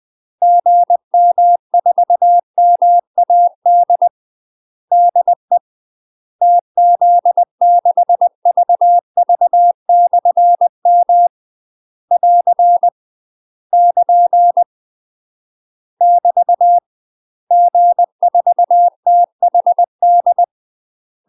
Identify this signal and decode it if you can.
Morse code, heard as a keyed tone, about 20 wpm, 700 Hz. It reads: GM4MAD DE TZ6VV/M <AR> <KN> <BT> G4THD
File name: UK MASTER QSOs TRAINING / 20WPM / UK MASTER QSOs 0032